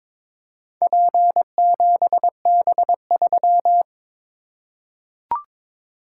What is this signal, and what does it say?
Morse code, heard as a keyed tone, about 22 wpm, 700 Hz. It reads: P7B3
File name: practice / 22wpm / Four Letter-Number Combinations - 22wpm